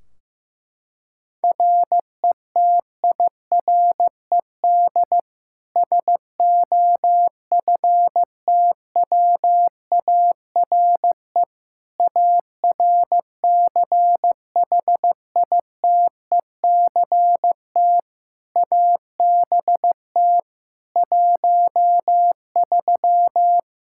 Morse code, heard as a keyed tone, about 15 wpm, 700 Hz. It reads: RETIRED SOFTWARE ARCHITECT ABT 13
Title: Practice QSO - Exchange 3 - More Info - 15wpm